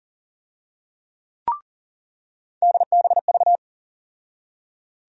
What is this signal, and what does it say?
Morse code, heard as a keyed tone, about 40 wpm, 700 Hz. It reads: DBV
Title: USA Call Sign Postfixes - 3 Letters 40wpm